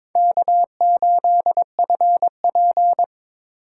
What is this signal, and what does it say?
Morse code, heard as a keyed tone, about 23 wpm, 700 Hz. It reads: K8FP
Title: USA Call Signs - 4 Characters - 3x Character Spacing - 22wpm